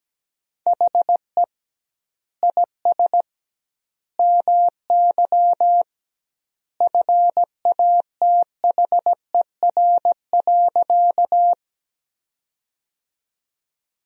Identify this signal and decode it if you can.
Morse code, heard as a keyed tone, about 17 wpm, 700 Hz. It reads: HE IS MY FATHER.
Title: 2x Word-Spacing - Sentences from Top 300 Words - 17wpm